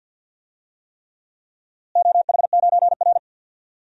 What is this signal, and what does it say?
Morse code, heard as a keyed tone, about 52 wpm, 700 Hz. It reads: OH9R
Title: DXCC-100 Call Signs - 4 Characters - Repeated 3x - 50wpm